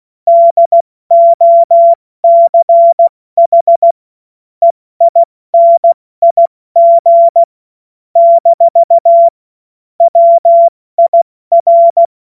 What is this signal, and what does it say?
Morse code, heard as a keyed tone, about 16 wpm, 680 Hz. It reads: DOCH EINIG - WIR
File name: Morsekurs / Witze / Witz 21. März 2023 70 F80